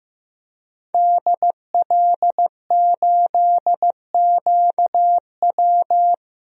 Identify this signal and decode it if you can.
Morse code, heard as a keyed tone, about 15 wpm, 700 Hz. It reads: DL8QW
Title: DXCC-100 Call Signs - 5 Characters - 15wpm